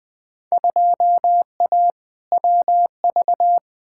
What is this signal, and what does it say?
Morse code, heard as a keyed tone, about 20 wpm, 700 Hz. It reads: 2A WV
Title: Field Day Exchanges Encore - 20wpm